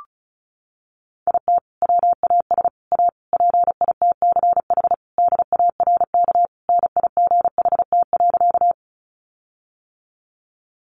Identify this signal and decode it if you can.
Morse code, heard as a keyed tone, about 34 wpm, 700 Hz. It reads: IT WAS A PITCH DARK NIGHT.
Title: Sentences from Top 1300 Words 35wpm